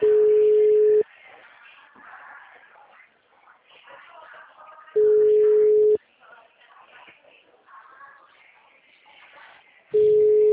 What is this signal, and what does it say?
Ringback tone (Europe, 425 Hz).